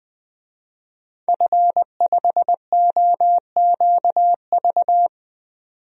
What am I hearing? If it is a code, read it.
Morse code, heard as a keyed tone, about 20 wpm, 700 Hz. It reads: F5OQV
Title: DXCC-100 Call Signs - 5 Characters - 20wpm